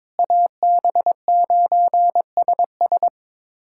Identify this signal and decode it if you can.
Morse code, heard as a keyed tone, about 22 wpm, 700 Hz. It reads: AB9SS
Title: USA Callsign DE Callsign - 5 Characters - 22wpm